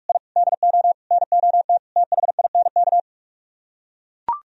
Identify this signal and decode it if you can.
Morse code, heard as a keyed tone, about 44 wpm, 700 Hz. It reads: I DO NOT THINK